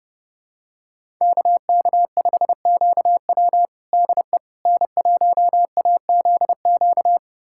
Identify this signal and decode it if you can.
Morse code, heard as a keyed tone, about 30 wpm, 700 Hz. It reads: KK5QW DE N1AZQ
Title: USA Callsign DE Callsign - 5 Characters - 30wpm